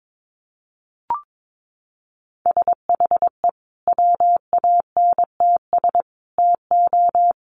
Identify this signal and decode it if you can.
Morse code, heard as a keyed tone, about 22 wpm, 700 Hz. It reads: SHE WANTS TO
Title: Easy Phrases 22wpm